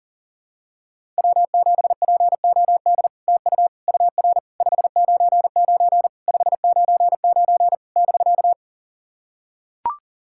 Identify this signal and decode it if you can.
Morse code, heard as a keyed tone, about 40 wpm, 700 Hz. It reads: W7POD TU UR 599 599 <BK>
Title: SOTA QSO Practice 40wpm